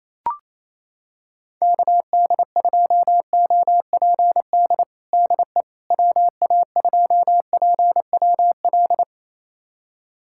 Morse code, heard as a keyed tone, about 28 wpm, 700 Hz. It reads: KD2OPD DE WA2PWL
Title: USA Callsign DE Callsign 28wpm